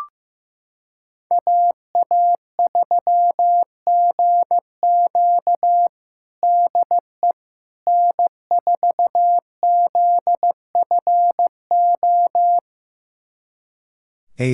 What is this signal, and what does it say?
Morse code, heard as a keyed tone, about 16 wpm, 700 Hz. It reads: AA3GQ DE N4ZFO
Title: USA Callsign DE Callsign - 5 Characters - 15wpm